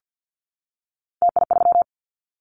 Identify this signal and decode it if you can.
Morse code, heard as a keyed tone, about 50 wpm, 690 Hz. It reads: TI3
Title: Callsign Prefixes for Top 100 DXCC - 50wpm